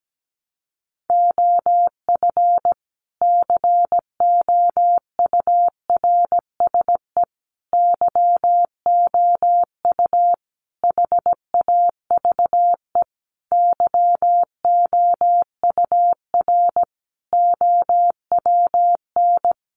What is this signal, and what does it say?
Morse code, heard as a keyed tone, about 17 wpm, 700 Hz. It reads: OF COURSE YOU HAVE YOUR OWN
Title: Sentences from Top 1200 Words 17wpm